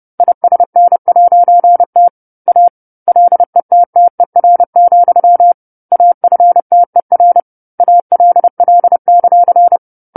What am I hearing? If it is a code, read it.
Morse code, heard as a keyed tone, about 30 wpm, 700 Hz. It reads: ISN'T A LETTER, AFTER ALL;
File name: 10-Chapter-10 30 WPM